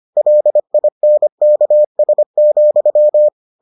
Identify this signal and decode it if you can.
Morse code, heard as a keyed tone, about 25 wpm, 600 Hz. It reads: LINKS,